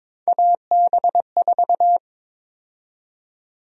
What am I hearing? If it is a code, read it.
Morse code, heard as a keyed tone, about 22 wpm, 700 Hz. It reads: AB4
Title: USA Call Sign Prefixes - Repeated 2x - 22wpm